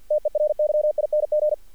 Morse code, heard as a keyed tone, about 48 wpm, 610 Hz, with radio noise. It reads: TERYING
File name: archived / CRSnet